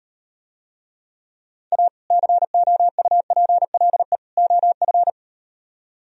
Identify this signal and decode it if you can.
Morse code, heard as a keyed tone, about 38 wpm, 700 Hz. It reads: A COUPLE OF